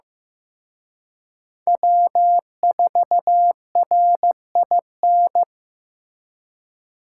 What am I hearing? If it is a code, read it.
Morse code, heard as a keyed tone, about 18 wpm, 700 Hz. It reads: W4RIN
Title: Speed Racing - USA 1x3 Call Signs 15wpm